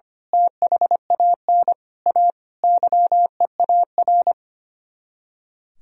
Morse code, heard as a keyed tone, about 24 wpm, 700 Hz. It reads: THAN A YEAR
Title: Speed-Racing - Sets of 5 Words 25wpm